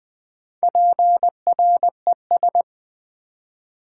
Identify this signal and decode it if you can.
Morse code, heard as a keyed tone, about 20 wpm, 700 Hz. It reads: PRES
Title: Build A Word - 4 Letters At A Time - 12-Letter Words - 20wpm